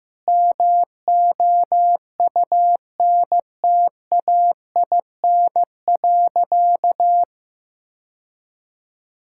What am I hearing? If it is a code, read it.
Morse code, heard as a keyed tone, about 15 wpm, 700 Hz. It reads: MOUNTAIN.
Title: Sentences from Top 1000 Words - Review Part 5 - 15wpm